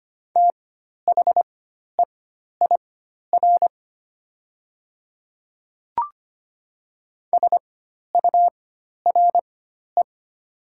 Morse code, heard as a keyed tone, about 25 wpm, spaced as for 12 wpm, 700 Hz. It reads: THEIR SURE
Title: 4x Character Spacing - Top 300 Words - 25wpm